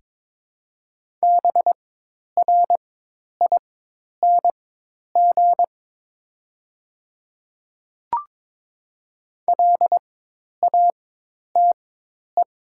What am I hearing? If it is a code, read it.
Morse code, heard as a keyed tone, about 22 wpm, spaced as for 10 wpm, 700 Hz. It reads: BRING LATE
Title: 4x Character Spacing - Top 500 Words - 22wpm